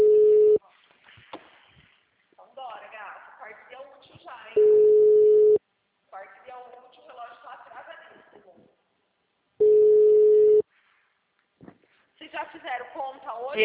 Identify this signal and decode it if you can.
Ringback tone (Europe, 425 Hz), with speech.